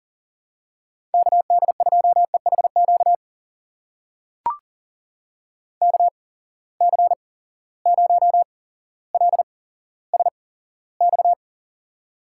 Morse code, heard as a keyed tone, about 40 wpm, spaced as for 17 wpm, 700 Hz. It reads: KD2EHQ KC0LSX
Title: USA Call Signs - 6 Characters - 8x Character Spacing - 40wpm